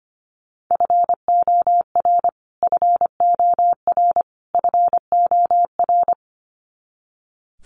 Morse code, heard as a keyed tone, about 25 wpm, 700 Hz. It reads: FOR FOR FOR
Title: Top 500 Words Repeated 25wpm